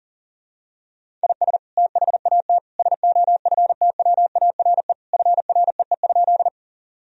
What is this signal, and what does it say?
Morse code, heard as a keyed tone, about 40 wpm, 700 Hz. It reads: IS THAT SOFTWARE FREE?
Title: Sentences from Top 1000 Words - Review Part 3 - 40wpm